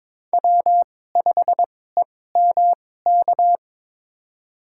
Morse code, heard as a keyed tone, about 22 wpm, spaced as for 16 wpm, 700 Hz. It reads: W5EMK
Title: USA Call Signs - 5 Characters - 2x Character Spacing - 22wpm